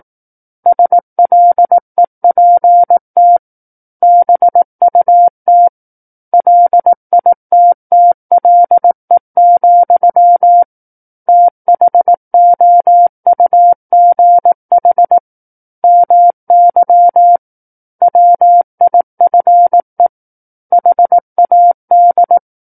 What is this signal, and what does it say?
Morse code, heard as a keyed tone, about 18 wpm, 700 Hz. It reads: SLEPT BUT LITTLE, THOUGH MY WIFE HAD
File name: 18 WPM